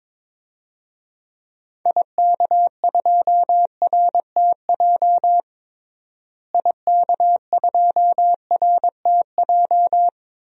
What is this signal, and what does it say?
Morse code, heard as a keyed tone, about 21 wpm, 700 Hz. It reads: IK2RTJ IK2RTJ
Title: DXCC-100 Call Signs - 6 Characters - Repeated 2x - 22wpm